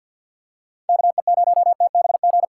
Morse code, heard as a keyed tone, about 52 wpm, 700 Hz. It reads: KE0TBG
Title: USA Call Signs - 2x Character Spacing - 50wpm